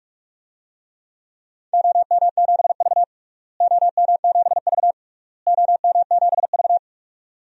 Morse code, heard as a keyed tone, about 46 wpm, 700 Hz. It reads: OM7V OM7V OM7V
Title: DXCC-100 Call Signs - 4 Characters - Repeated 3x - 45wpm